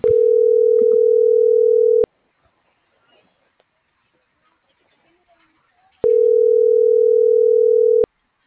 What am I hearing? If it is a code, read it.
Ringback tone (North America), 440 + 480 Hz.